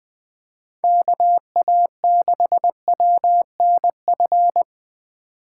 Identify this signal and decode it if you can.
Morse code, heard as a keyed tone, about 20 wpm, 700 Hz. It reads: KA6WNF